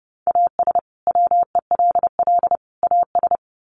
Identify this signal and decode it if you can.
Morse code, heard as a keyed tone, about 30 wpm, 700 Hz. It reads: AS WELL AS
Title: Sets of 3 Words 30wpm